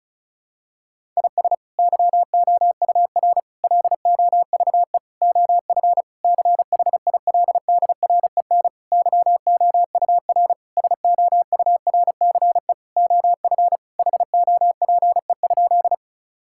Morse code, heard as a keyed tone, about 34 wpm, 700 Hz. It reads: IS YOUR LOVE OF CHILDREN YOUR SOURCE OF HOPE?
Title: Sentences from Top 1000 Words - Review Part 5 - 35wpm